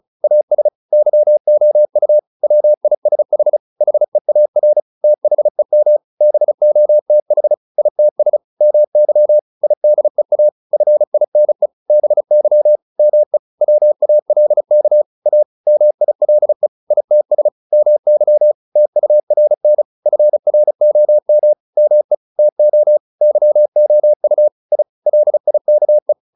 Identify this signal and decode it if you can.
Morse code, heard as a keyed tone, about 34 wpm, 600 Hz. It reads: AS YOU WISH HEAR THEM BOTH ITS MY IDEA FINE BY ME WALK A MILE ITS MY TURN FROM ME TO YOU I LIKE